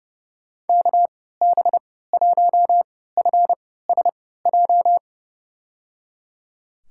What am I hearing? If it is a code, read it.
Morse code, heard as a keyed tone, about 32 wpm, spaced as for 17 wpm, 700 Hz. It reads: KB1FSJ
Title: USA Call Signs - 3x Character Spacing - 30wpm